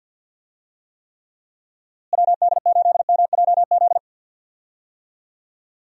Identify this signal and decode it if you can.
Morse code, heard as a keyed tone, about 50 wpm, 700 Hz. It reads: WD8MJZ
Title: USA Call Signs - 6 Characters - Repeated 3x - 50wpm